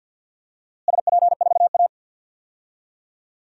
Morse code, heard as a keyed tone, about 51 wpm, 700 Hz. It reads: SP4A
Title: DXCC-100 Call Signs - 4 Characters - Repeated 3x - 50wpm